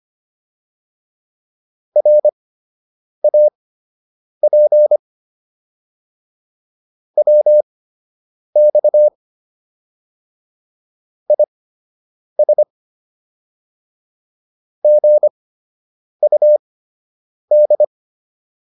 Morse code, heard as a keyed tone, about 25 wpm, spaced as for 8 wpm, 600 Hz. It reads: RAP WX IS GUD